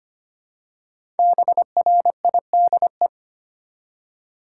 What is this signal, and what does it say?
Morse code, heard as a keyed tone, about 25 wpm, 700 Hz. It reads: BRIDE
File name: 5 Letter Words - First 3 Letters Given - 25wpm